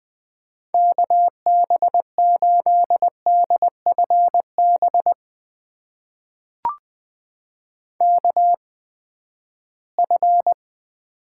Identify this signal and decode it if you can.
Morse code, heard as a keyed tone, about 20 wpm, spaced as for 14 wpm, 700 Hz. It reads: KB8DFB KF
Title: USA Call Signs - 6 Characters - 8x Character Spacing - 20wpm